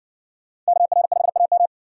Morse code, heard as a keyed tone, about 57 wpm, 700 Hz. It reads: KR5AM